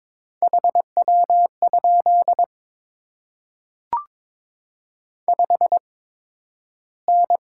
Morse code, heard as a keyed tone, about 22 wpm, spaced as for 12 wpm, 700 Hz. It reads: HW? 5N
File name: Top 20 QSO Elements - 8x Character Spacing - 22wpm